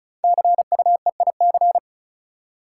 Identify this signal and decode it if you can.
Morse code, heard as a keyed tone, about 36 wpm, 690 Hz. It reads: CUEIC
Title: Code Groups - 5 Letter:Numbers - 35wpm